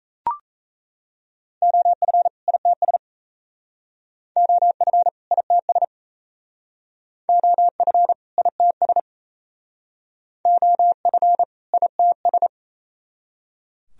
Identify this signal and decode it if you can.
Morse code, heard as a keyed tone, about 35 wpm, 700 Hz. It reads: OF ITS OF ITS OF ITS OF ITS
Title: Speed-Racing - Sets of 2 Words 28wpm